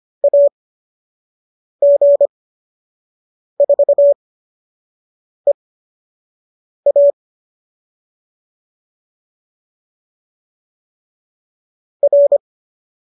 Morse code, heard as a keyed tone, about 25 wpm, spaced as for 6 wpm, 570 Hz. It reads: AG4EA R